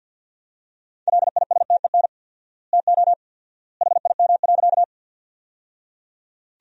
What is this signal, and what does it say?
Morse code, heard as a keyed tone, about 50 wpm, 700 Hz. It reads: LISTEN TO HIM.
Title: 4x Word-Spacing - Sentences from Top 400 Words - 50wpm